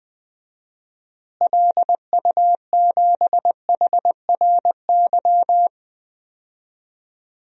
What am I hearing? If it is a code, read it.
Morse code, heard as a keyed tone, about 20 wpm, 700 Hz. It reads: LU7HRY
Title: DXCC-100 Call Signs - 6 Characters - Repeated 3x - 20wpm